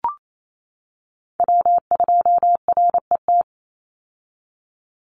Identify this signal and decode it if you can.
Morse code, heard as a keyed tone, about 28 wpm, 700 Hz. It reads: W2RET
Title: USA Call Signs 28wpm